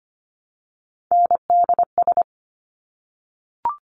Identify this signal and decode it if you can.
Morse code, heard as a keyed tone, about 25 wpm, 700 Hz. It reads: NDS
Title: Top Three Letter Combinations in Words 25wpm